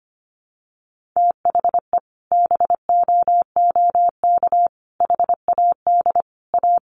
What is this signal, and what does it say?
Morse code, heard as a keyed tone, about 25 wpm, 700 Hz. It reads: THE BOOK HAD A
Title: Sentences from Top 1400 Words 25wpm